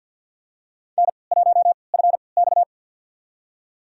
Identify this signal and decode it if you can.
Morse code, heard as a keyed tone, about 46 wpm, spaced as for 27 wpm, 700 Hz. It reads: N1FX
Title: USA Call Signs - 4 Characters - 3x Character Spacing - 50wpm